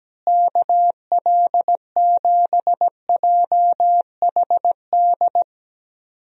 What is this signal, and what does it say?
Morse code, heard as a keyed tone, about 17 wpm, 700 Hz. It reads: KL7JHD
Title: USA Call Signs - 6 Characters - Repeated 3x - 17wpm